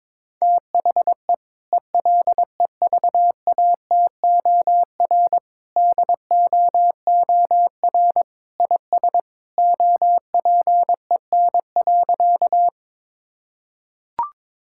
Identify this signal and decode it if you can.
Morse code, heard as a keyed tone, about 22 wpm, 700 Hz. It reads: THE ELEVATOR DOOR IS OPEN.
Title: Sentences from Top 1600 Words 22wpm